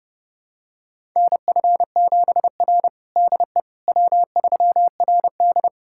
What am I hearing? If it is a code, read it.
Morse code, heard as a keyed tone, about 32 wpm, 700 Hz. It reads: NF7R DE W3RD
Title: USA Callsign DE Callsign - 4 Characters - 30wpm